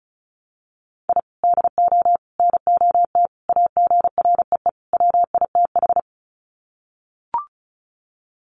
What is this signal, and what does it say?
Morse code, heard as a keyed tone, about 34 wpm, 700 Hz. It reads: I DO NOT AGREE WITH
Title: Sets of 5 Words Encore 35wpm